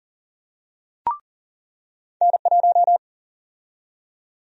Morse code, heard as a keyed tone, about 40 wpm, 700 Hz. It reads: N1
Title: USA Call Sign Prefixes - 40wpm